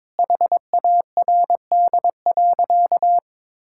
Morse code, heard as a keyed tone, about 22 wpm, 700 Hz. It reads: HARD.
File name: practice / 22wpm / Sentences with 2-Word Phrases - 22wpm